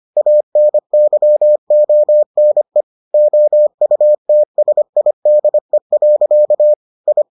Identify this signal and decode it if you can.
Morse code, heard as a keyed tone, about 25 wpm, 600 Hz. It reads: ANYONE OUTSIDE. I